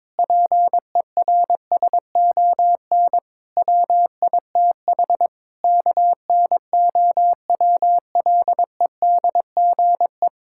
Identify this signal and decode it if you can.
Morse code, heard as a keyed tone, about 22 wpm, 700 Hz. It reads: PERSON WITH KNOWLEDGE
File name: Sentences with 9-Word Phrases - 22wpm